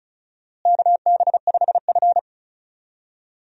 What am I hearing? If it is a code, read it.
Morse code, heard as a keyed tone, about 36 wpm, 700 Hz. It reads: KB5F
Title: USA Call Signs - 4 Characters - Repeated 3x - 35wpm